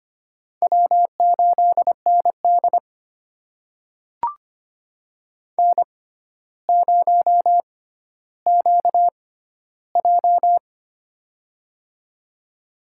Morse code, heard as a keyed tone, about 25 wpm, spaced as for 13 wpm, 700 Hz. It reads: W8ND N0QJ
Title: USA Call Signs - 4 Characters - 6x Character Spacing - 25wpm